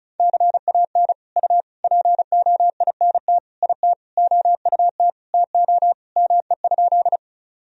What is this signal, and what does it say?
Morse code, heard as a keyed tone, about 35 wpm, 700 Hz. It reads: CAN U POINT IT OUT TO ME?
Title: Sentences from Top 200 Words and Abbreviations 35wpm